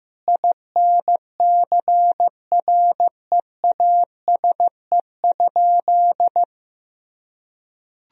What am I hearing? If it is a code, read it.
Morse code, heard as a keyed tone, about 15 wpm, 700 Hz. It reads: INCREASE?
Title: Sentences from Top 1000 Words - Review Part 4 - 15wpm